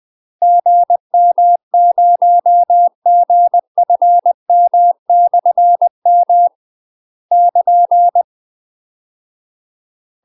Morse code, heard as a keyed tone, about 20 wpm, 700 Hz. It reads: GM0GFM/M <KN>
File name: UK MASTER QSOs TRAINING / 20WPM / UK MASTER QSOs 0047